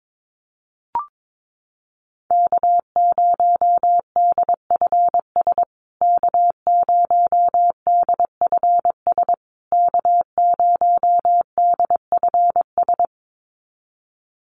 Morse code, heard as a keyed tone, about 22 wpm, 700 Hz. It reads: K0DFS K0DFS K0DFS
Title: USA Call Signs Repeated 3x 22wpm